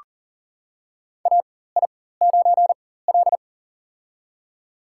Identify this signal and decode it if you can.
Morse code, heard as a keyed tone, about 40 wpm, spaced as for 19 wpm, 700 Hz. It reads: AI9L
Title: USA Call Signs - 4 Characters - 4x Character Spacing - 40wpm